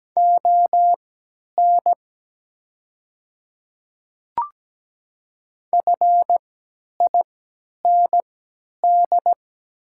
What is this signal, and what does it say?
Morse code, heard as a keyed tone, about 17 wpm, spaced as for 10 wpm, 700 Hz. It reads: ON FIND
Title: 3x Character Spacing - Top 500 Words - 17wpm